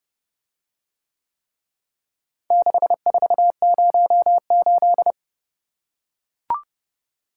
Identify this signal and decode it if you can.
Morse code, heard as a keyed tone, about 31 wpm, 700 Hz. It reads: 6408